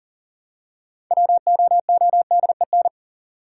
Morse code, heard as a keyed tone, about 40 wpm, 700 Hz. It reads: WOODEN
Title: Six Letter Words - 40wpm